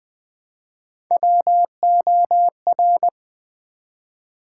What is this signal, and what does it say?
Morse code, heard as a keyed tone, about 20 wpm, 700 Hz. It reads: WOR
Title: Build A Word - 1 Letter At A Time - 5-Letter Words - 20wpm